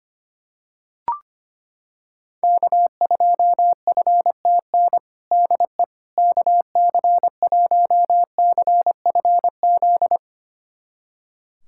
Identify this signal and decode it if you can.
Morse code, heard as a keyed tone, about 25 wpm, 700 Hz. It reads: K2FTN DE KC1CFZ
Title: USA Callsign DE Callsign 25wpm